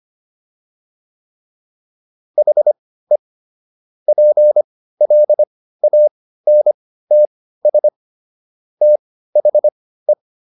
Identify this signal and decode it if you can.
Morse code, heard as a keyed tone, about 25 wpm, spaced as for 15 wpm, 600 Hz. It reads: HE PLANTS THE